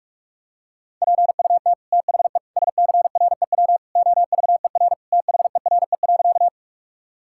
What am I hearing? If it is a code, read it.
Morse code, heard as a keyed tone, about 46 wpm, 700 Hz. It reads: PUT THE SCREW OVER THERE.
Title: Sentences from Top 1600 Words 45wpm